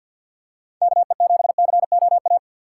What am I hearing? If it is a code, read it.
Morse code, heard as a keyed tone, about 52 wpm, 700 Hz. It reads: KE7COA
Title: USA Call Signs - 6 Characters - 3x Character Spacing - 50wpm